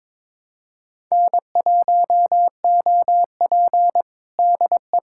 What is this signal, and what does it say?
Morse code, heard as a keyed tone, about 22 wpm, 700 Hz. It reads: N1OP DE
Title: USA Callsign DE Callsign - 4 Characters - 22wpm